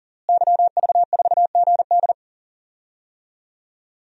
Morse code, heard as a keyed tone, about 40 wpm, 700 Hz. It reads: YV4GD
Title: DXCC-100 Call Signs - 5 Characters - Repeated 2x - 40wpm